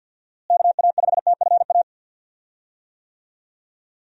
Morse code, heard as a keyed tone, about 51 wpm, 700 Hz. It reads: KA5TFA
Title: USA Call Signs - 6 Characters - Repeated 2x - 50wpm